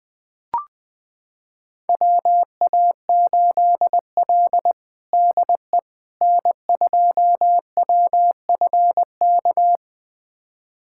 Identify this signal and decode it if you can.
Morse code, heard as a keyed tone, about 21 wpm, 700 Hz. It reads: WA8L DE N2WFK
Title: USA Callsign DE Callsign 20wpm